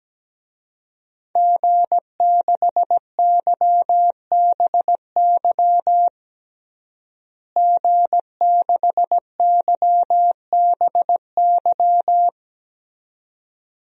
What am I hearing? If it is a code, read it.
Morse code, heard as a keyed tone, about 17 wpm, 700 Hz. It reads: G6YBY G6YBY
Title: DXCC-100 Call Signs - 5 Characters - Repeated 2x - 17wpm